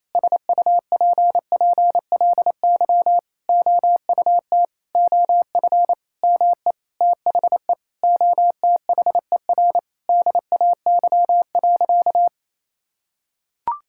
Morse code, heard as a keyed tone, about 28 wpm, 700 Hz. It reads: SUPPLY OUT OF ME THE OTHER DAY.
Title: Sentences from Top 1000 Words - Review Part 5 - 28wpm